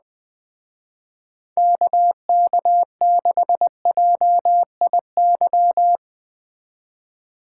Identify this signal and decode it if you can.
Morse code, heard as a keyed tone, about 20 wpm, 700 Hz. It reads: KK6JIY
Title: Speed Racing - USA 2x3 Call Signs 17wpm